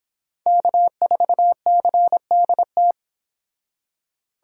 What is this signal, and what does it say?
Morse code, heard as a keyed tone, about 26 wpm, 700 Hz. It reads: K4CDT